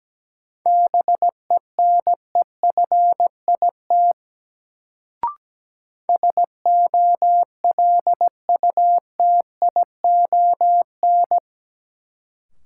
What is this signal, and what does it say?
Morse code, heard as a keyed tone, about 17 wpm, 700 Hz. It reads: BENEFIT SOLUTION
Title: Sentences from Top 1300 Words - New Word Drill Alt - 17wpm